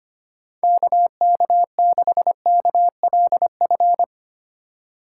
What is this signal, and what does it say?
Morse code, heard as a keyed tone, about 25 wpm, 700 Hz. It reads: KK6KLF
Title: USA Call Signs - 6 Characters - 4x Character Spacing - 25wpm